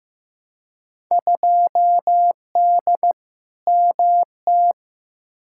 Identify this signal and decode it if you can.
Morse code, heard as a keyed tone, about 15 wpm, 700 Hz. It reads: 2D MT
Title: Field Day Exchanges Encore - 15wpm